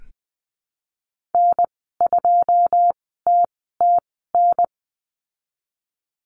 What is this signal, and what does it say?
Morse code, heard as a keyed tone, about 20 wpm, spaced as for 15 wpm, 700 Hz. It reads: N2TTN
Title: 122 Lesson 20g - Number 3 - Call Signs 20wpm